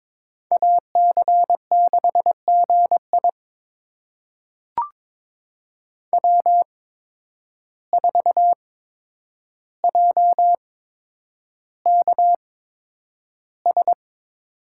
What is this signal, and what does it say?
Morse code, heard as a keyed tone, about 22 wpm, spaced as for 9 wpm, 700 Hz. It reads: AC6GI W4JKS
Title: USA Call Signs - 5 Characters - 8x Character Spacing - 22wpm